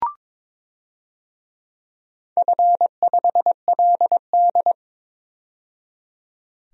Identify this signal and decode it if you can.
Morse code, heard as a keyed tone, about 22 wpm, 700 Hz. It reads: F5LD
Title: DXCC-100 Call Signs - 4 Characters - 22wpm